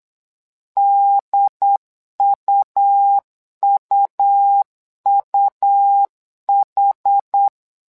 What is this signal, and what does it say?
Morse code, heard as a keyed tone, about 8 wpm, 800 Hz. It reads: DUUUH
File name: LEZIONI / 2 UHDN / 2 UHD 8wpm 2258